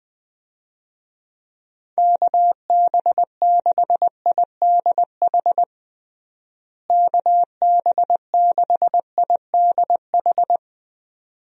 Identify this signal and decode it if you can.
Morse code, heard as a keyed tone, about 20 wpm, 700 Hz. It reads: KB6IDH KB6IDH
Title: USA Call Signs - 6 Characters - Repeated 2x - 20wpm